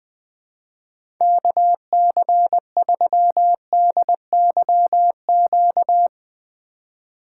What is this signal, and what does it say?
Morse code, heard as a keyed tone, about 20 wpm, 700 Hz. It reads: KC3DYQ
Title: USA Call Signs - 6 Characters - 2x Character Spacing - 20wpm